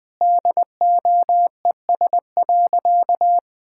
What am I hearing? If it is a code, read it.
Morse code, heard as a keyed tone, about 20 wpm, 700 Hz. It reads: DOES.
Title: Sentences with 2-Word Phrases - Spoken Beginning - 20wpm